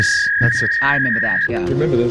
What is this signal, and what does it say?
Fax machine handshake, with speech.